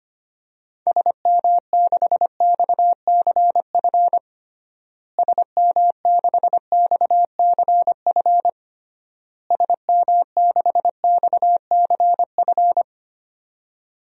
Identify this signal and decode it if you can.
Morse code, heard as a keyed tone, about 25 wpm, 700 Hz. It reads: SM6XCF SM6XCF SM6XCF
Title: DXCC-100 Call Signs - 6 Characters - Repeated 3x - 25wpm